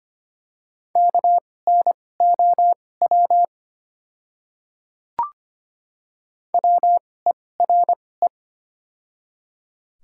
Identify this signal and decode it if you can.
Morse code, heard as a keyed tone, about 25 wpm, spaced as for 18 wpm, 700 Hz. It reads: KNOW WERE
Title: 2x Character Spacing - Top 300 Words - 25wpm